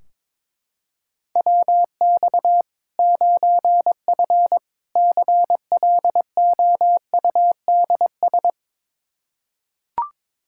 Morse code, heard as a keyed tone, about 22 wpm, 700 Hz. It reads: WX 9F CLOUDS
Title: Practice QSO - Exchange 2 - Info - 22wpm